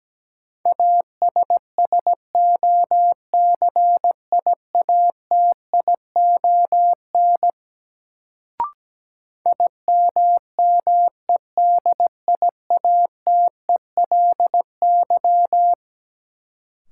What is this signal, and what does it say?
Morse code, heard as a keyed tone, about 17 wpm, 700 Hz. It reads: ASSOCIATION IMMEDIATELY
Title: Sentences from Top 1100 Words - New Word Drill - 17wpm